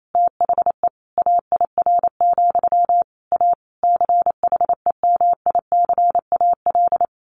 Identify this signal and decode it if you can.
Morse code, heard as a keyed tone, about 28 wpm, 700 Hz. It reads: THE AIR, A CHEMICAL
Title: A Sound Of Thunder 28wpm